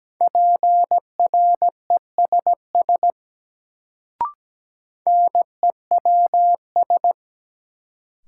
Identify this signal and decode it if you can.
Morse code, heard as a keyed tone, about 17 wpm, 700 Hz. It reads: PRESS NEWS
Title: Sentences from Top 800 Words - New Word Drill - 17wpm